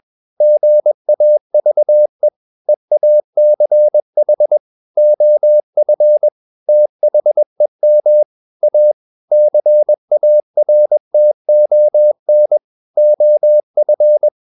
Morse code, heard as a keyed tone, about 21 wpm, 600 Hz. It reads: GAVE EACH OF THEM A CARTON OF
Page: CW med Gnister